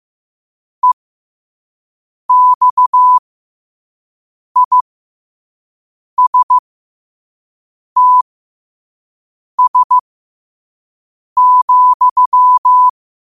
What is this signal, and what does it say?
Morse code, heard as a keyed tone, about 15 wpm, spaced as for 5 wpm, 1000 Hz. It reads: EXISTS,
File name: qotdfiles / 5wpm